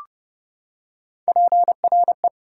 Morse code, heard as a keyed tone, about 31 wpm, 700 Hz. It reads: PRE
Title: Prefixes and Suffixes 30wpm